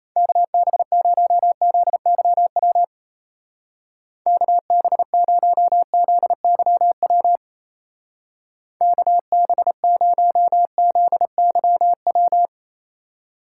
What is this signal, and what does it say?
Morse code, heard as a keyed tone, about 37 wpm, 700 Hz. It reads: KB0ZYW KB0ZYW KB0ZYW